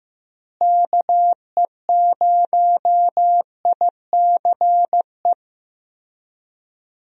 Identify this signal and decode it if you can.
Morse code, heard as a keyed tone, about 15 wpm, 700 Hz. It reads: KE0ICE
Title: USA Call Signs - 6 Characters - Repeated 3x - 15wpm